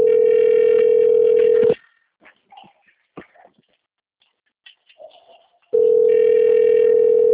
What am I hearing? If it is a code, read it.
Ringback tone (North America), 440 + 480 Hz.
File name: out- - - - - .18922